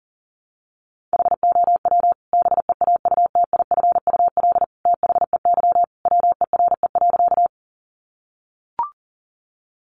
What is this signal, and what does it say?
Morse code, heard as a keyed tone, about 40 wpm, 700 Hz. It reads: HOW BEAUTIFUL THEY WERE.
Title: Sentences from Top 900 Words 40wpm